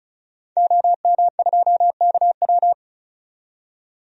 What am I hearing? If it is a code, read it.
Morse code, heard as a keyed tone, about 36 wpm, 700 Hz. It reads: OM2KW
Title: DXCC-100 Call Signs - 5 Characters - 35wpm